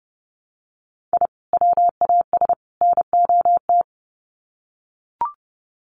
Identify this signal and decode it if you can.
Morse code, heard as a keyed tone, about 30 wpm, 700 Hz. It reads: I WAS NOT
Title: Sets of 3 Words 30wpm